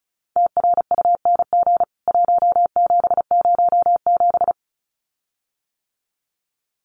Morse code, heard as a keyed tone, about 34 wpm, 700 Hz. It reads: TRUNG 1707
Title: CWT Exchange 35wpm